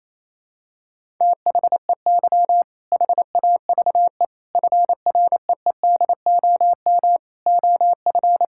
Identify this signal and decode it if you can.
Morse code, heard as a keyed tone, about 28 wpm, 700 Hz. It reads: THEY HAVE FREEDOM OF
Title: Sentences from Top 1000 Words - Review Part 5 - 28wpm